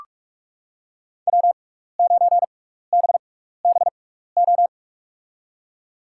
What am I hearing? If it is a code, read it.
Morse code, heard as a keyed tone, about 46 wpm, spaced as for 16 wpm, 700 Hz. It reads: W9BBO
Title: USA Call Signs - 5 Characters - 6x Character Spacing - 45wpm